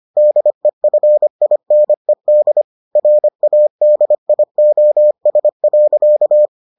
Morse code, heard as a keyed tone, about 25 wpm, 600 Hz. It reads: DEFINED RADIOS.